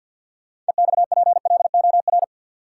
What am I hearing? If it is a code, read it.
Morse code, heard as a keyed tone, about 51 wpm, 700 Hz. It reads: EXPLOR
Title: Sentences from Top 1100 Words - New Word Drill Alt - 50wpm